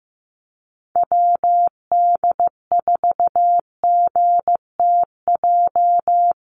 Morse code, heard as a keyed tone, about 15 wpm, 700 Hz. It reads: WD4GTJ
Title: USA Call Signs Repeated 2x 15wpm